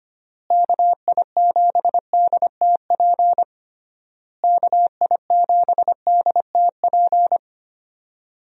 Morse code, heard as a keyed tone, about 25 wpm, 700 Hz. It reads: KI7DTP KI7DTP
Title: USA Call Signs - 6 Characters - Repeated 3x - 25wpm